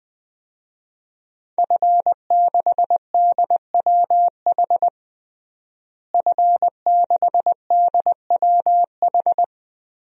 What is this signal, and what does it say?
Morse code, heard as a keyed tone, about 20 wpm, 700 Hz. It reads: F6DWH F6DWH
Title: DXCC-100 Call Signs - 5 Characters - Repeated 2x - 20wpm